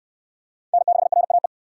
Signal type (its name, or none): Morse code